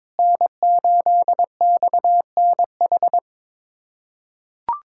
Morse code, heard as a keyed tone, about 22 wpm, 700 Hz. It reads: N8XNH
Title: USA Callsign DE Callsign - 5 Characters - 22wpm